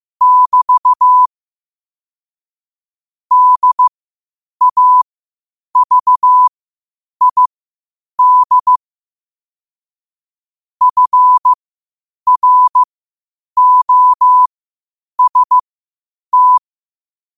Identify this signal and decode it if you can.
Morse code, heard as a keyed tone, about 15 wpm, spaced as for 8 wpm, 1000 Hz. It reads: <BT> DAVID FROST
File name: qotdfiles / 8wpm